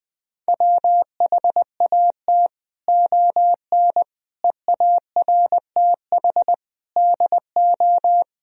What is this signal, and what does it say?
Morse code, heard as a keyed tone, about 20 wpm, 700 Hz. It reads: WHAT ON EARTH DO
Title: Sentences from Top 200 Words and Abbreviations 20wpm